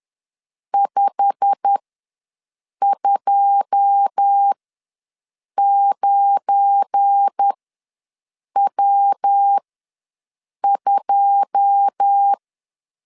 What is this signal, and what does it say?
Morse code, heard as a keyed tone, about 10 wpm, spaced as for 6 wpm, 800 Hz. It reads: 529W2
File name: LEZIONI / 9 2 ws 10wpm 4130